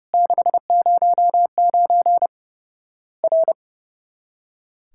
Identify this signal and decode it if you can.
Morse code, heard as a keyed tone, about 30 wpm, 700 Hz. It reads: 609 R
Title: Worked All Europe DX Contest - QTC Exchange - 30wpm